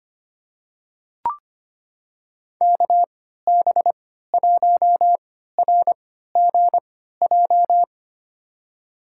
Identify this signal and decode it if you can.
Morse code, heard as a keyed tone, about 25 wpm, spaced as for 14 wpm, 700 Hz. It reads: KB1RGJ
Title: USA Call Signs - 3x Character Spacing - 25wpm